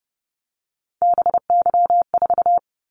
Morse code, heard as a keyed tone, about 29 wpm, 700 Hz. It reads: BY4
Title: Callsign Prefixes for Top 100 DXCC - 30wpm